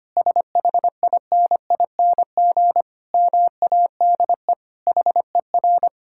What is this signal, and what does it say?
Morse code, heard as a keyed tone, about 25 wpm, 700 Hz. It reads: SHINING MADE HER
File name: practice / 25wpm / Sentences with 4-Word Phrases - 25wpm